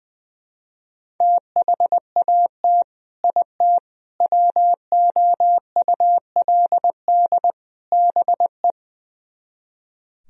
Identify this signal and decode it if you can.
Morse code, heard as a keyed tone, about 20 wpm, 700 Hz. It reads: THAT IT WOULD BE
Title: Speed-Racing - Sets of 4 Words 20wpm